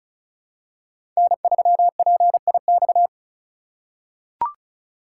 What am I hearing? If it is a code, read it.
Morse code, heard as a keyed tone, about 34 wpm, 700 Hz. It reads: N3PIX
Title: USA Call Signs - 5 Characters - 3x Character Spacing - 35wpm